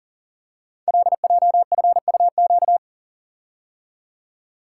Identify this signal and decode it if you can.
Morse code, heard as a keyed tone, about 42 wpm, 700 Hz. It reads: LJFUQ
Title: Code Groups - 5 Letters - 40wpm